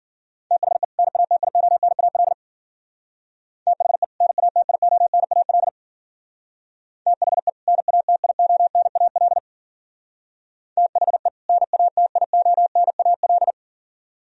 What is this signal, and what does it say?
Morse code, heard as a keyed tone, about 52 wpm, 700 Hz. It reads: THE NATIONAL THE NATIONAL THE NATIONAL THE NATIONAL